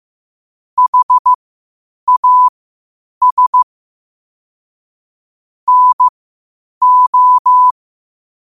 Morse code, heard as a keyed tone, about 15 wpm, spaced as for 8 wpm, 1000 Hz. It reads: HAS NO
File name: qotdfiles / 8wpm